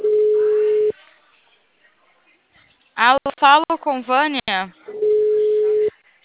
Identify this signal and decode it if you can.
Ringback tone (Europe, 425 Hz), with speech.
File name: out- -2015- - - .31794